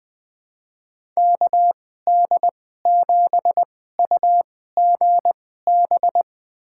Morse code, heard as a keyed tone, about 20 wpm, spaced as for 14 wpm, 700 Hz. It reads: KD7UGB
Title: USA Call Signs - 6 Characters - 2x Character Spacing - 20wpm